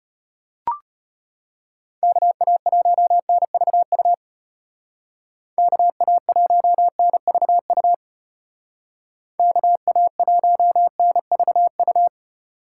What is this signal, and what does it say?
Morse code, heard as a keyed tone, about 33 wpm, 700 Hz. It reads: KA1NVU KA1NVU KA1NVU